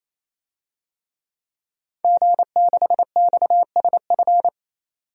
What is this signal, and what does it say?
Morse code, heard as a keyed tone, about 27 wpm, 700 Hz. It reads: G6XSF
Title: DXCC-100 Call Signs - 5 Characters - 28wpm